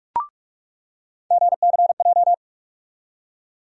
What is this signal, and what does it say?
Morse code, heard as a keyed tone, about 47 wpm, 700 Hz. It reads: GCJ
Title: USA Call Sign Postfixes - 3 Letters 45wpm